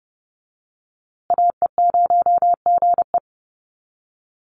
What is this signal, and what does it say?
Morse code, heard as a keyed tone, about 29 wpm, 700 Hz. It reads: AE0GE
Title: USA Call Signs Repeated 2x 30wpm